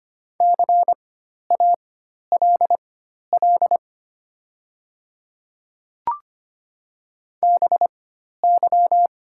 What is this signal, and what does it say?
Morse code, heard as a keyed tone, about 25 wpm, spaced as for 12 wpm, 700 Hz. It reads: CALL BY
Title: 4x Character Spacing - Top 100 Words - 25wpm